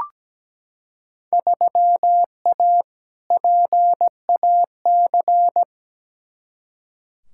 Morse code, heard as a keyed tone, about 17 wpm, 700 Hz. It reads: 3A PAC
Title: Field Day Exchanges Encore - 17wpm